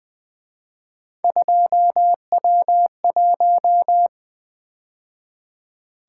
Morse code, heard as a keyed tone, about 23 wpm, 700 Hz. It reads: 2W1
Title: Speed-Racing - Three Letter-Number Combinations - 20wpm